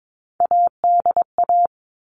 Morse code, heard as a keyed tone, about 22 wpm, 700 Hz. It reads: ADA
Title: All Cut Numbers - 3-Digit 22wpm